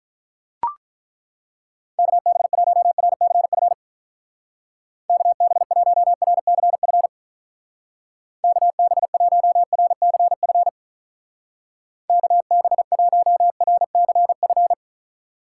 Morse code, heard as a keyed tone, about 42 wpm, 700 Hz. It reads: KB1RCF KB1RCF KB1RCF KB1RCF
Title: Speed Racing - USA 2x3 Call Signs 35wpm